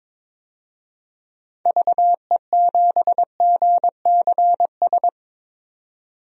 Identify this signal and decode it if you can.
Morse code, heard as a keyed tone, about 22 wpm, 700 Hz. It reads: VE7GCS
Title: DXCC-100 Call Signs - 6 Characters - 22wpm